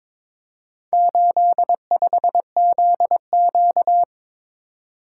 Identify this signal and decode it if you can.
Morse code, heard as a keyed tone, about 22 wpm, 700 Hz. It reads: 85ZQ
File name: practice / 22wpm / Four Letter-Number Combinations - 22wpm